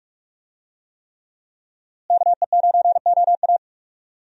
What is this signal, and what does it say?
Morse code, heard as a keyed tone, about 48 wpm, 700 Hz. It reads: KE9OA
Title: USA Call Signs - 5 Characters - Repeated 3x - 45wpm